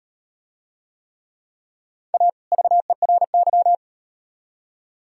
Morse code, heard as a keyed tone, about 38 wpm, 700 Hz. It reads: A VERY